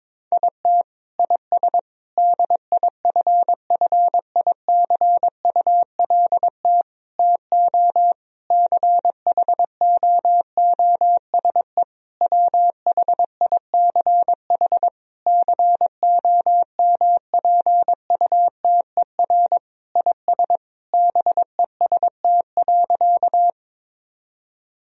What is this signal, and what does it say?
Morse code, heard as a keyed tone, about 22 wpm, 700 Hz. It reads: IT IS DIFFICULT TO CHOOSE WHICH COMPUTER IS BEST.
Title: Sentences from Top 1000 Words - Review Part 3 - 22wpm